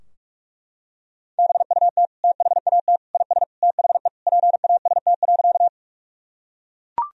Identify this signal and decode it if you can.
Morse code, heard as a keyed tone, about 41 wpm, 700 Hz. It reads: BUT THAT IS THE PAST.
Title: Sentences from Top 1000 Words - Review Part 3 - 45wpm